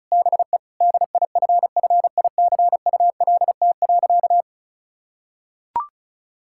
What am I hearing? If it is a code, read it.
Morse code, heard as a keyed tone, about 34 wpm, 700 Hz. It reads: BE DIFFICULT.
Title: Sentences from Top 1600 Words 35wpm